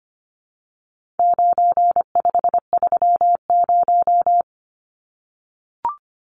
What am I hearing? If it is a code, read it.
Morse code, heard as a keyed tone, about 25 wpm, 700 Hz. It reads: 9530
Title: Numbers - 4-Digit 25wpm